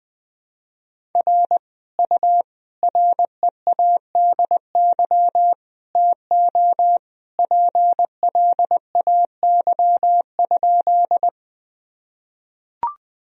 Morse code, heard as a keyed tone, about 21 wpm, 700 Hz. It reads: R U READY TO PLAY?
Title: Sentences from Top 400 Words and Abbreviations 20wpm